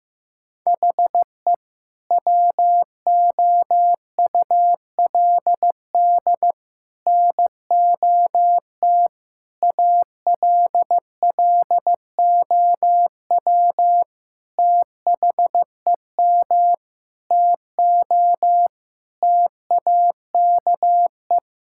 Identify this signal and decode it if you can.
Morse code, heard as a keyed tone, about 15 wpm, 700 Hz. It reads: HE WOULD NOT ALLOW THEM TO TAKE
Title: Sentences from Top 1000 Words - Review Part 5 - 15wpm